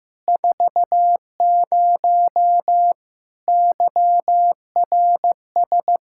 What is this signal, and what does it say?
Morse code, heard as a keyed tone, about 15 wpm, 700 Hz. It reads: 40 YRS
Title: Practice QSO - Exchange 3 - More Info - 15wpm